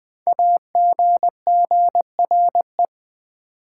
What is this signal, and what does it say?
Morse code, heard as a keyed tone, about 20 wpm, 700 Hz. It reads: AGGRE
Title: Build A Word - 5 Letters At A Time - 10-Letter Words - 20wpm